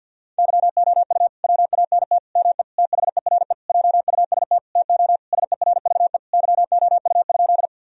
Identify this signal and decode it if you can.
Morse code, heard as a keyed tone, about 53 wpm, 700 Hz. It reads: YOU WANT ME THERE JUST TO SERVE YOU?
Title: Sentences from Top 1000 Words - Review Part 2 - 50wpm